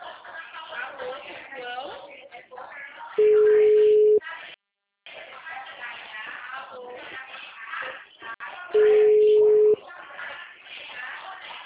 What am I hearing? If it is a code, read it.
Ringback tone (Europe, 425 Hz), with speech.
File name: out- -2010- - - .89607